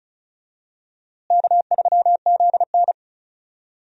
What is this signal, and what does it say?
Morse code, heard as a keyed tone, about 36 wpm, 700 Hz. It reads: K3ZN
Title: USA Call Signs - 4 Characters - Repeated 2x - 35wpm